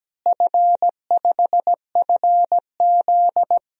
Morse code, heard as a keyed tone, about 17 wpm, 700 Hz. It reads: F5FZ
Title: DXCC-100 Call Signs - 4 Characters - Repeated 2x - 17wpm